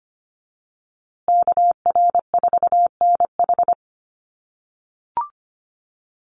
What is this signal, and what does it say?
Morse code, heard as a keyed tone, about 25 wpm, 700 Hz. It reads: KR4NH
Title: USA Call Signs Repeated 3x 25wpm